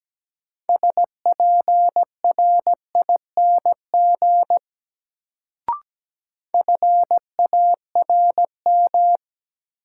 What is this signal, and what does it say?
Morse code, heard as a keyed tone, about 17 wpm, 700 Hz. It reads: SPRING FARM
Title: Sentences from Top 1200 Words - New Word Drill Alt - 17wpm